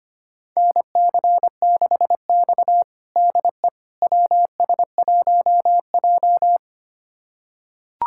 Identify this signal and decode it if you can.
Morse code, heard as a keyed tone, about 25 wpm, 700 Hz. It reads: NC6X DE WS1J
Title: USA Callsign DE Callsign - 4 Characters - 25wpm